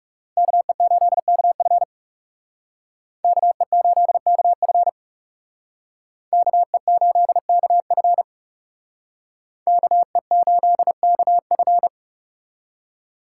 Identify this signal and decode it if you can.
Morse code, heard as a keyed tone, about 36 wpm, 700 Hz. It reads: KE8KF KE8KF KE8KF KE8KF
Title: Speed Racing - USA 2x2 Call Signs 30wpm